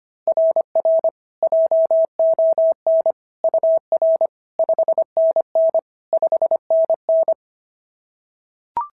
Morse code, heard as a keyed tone, about 22 wpm, 650 Hz. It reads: RR JON UR 5NN 5NN
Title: SOTA QSO Practice 25wpm